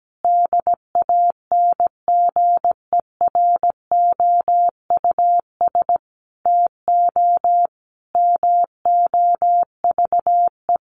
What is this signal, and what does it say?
Morse code, heard as a keyed tone, about 17 wpm, 700 Hz. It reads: DANGEROUS TO MOVE
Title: Sentences from Top 1200 Words 17wpm